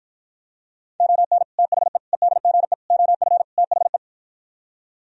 Morse code, heard as a keyed tone, about 52 wpm, 700 Hz. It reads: ON THE EDGE OF THE